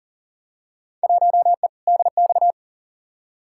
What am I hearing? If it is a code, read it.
Morse code, heard as a keyed tone, about 40 wpm, 700 Hz. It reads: 1E DX
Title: Field Day Exchanges Encore - 40wpm